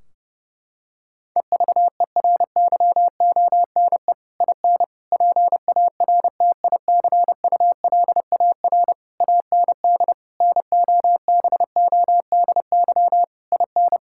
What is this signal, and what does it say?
Morse code, heard as a keyed tone, about 30 wpm, 700 Hz. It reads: EVERYONE IN PARTICULAR AND NOBODY IN
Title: Sentences from Top 1600 Words 30wpm